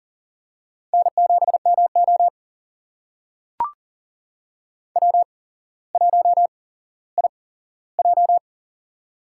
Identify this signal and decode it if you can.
Morse code, heard as a keyed tone, about 40 wpm, spaced as for 17 wpm, 700 Hz. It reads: N7MO W1IJ
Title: USA Call Signs - 4 Characters - 8x Character Spacing - 40wpm